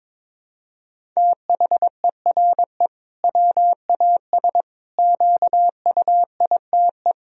Morse code, heard as a keyed tone, about 22 wpm, 700 Hz. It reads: THERE WAS QUITE
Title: Sentences from Top 1000 Words - Review Part 1 - 22wpm